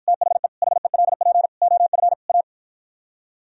Morse code, heard as a keyed tone, about 52 wpm, 700 Hz. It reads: THE HELP OF A